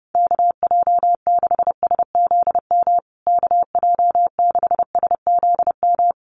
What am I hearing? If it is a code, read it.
Morse code, heard as a keyed tone, about 30 wpm, 700 Hz. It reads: KJ6SZM KJ6SZM
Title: USA Call Signs Repeated 2x 30wpm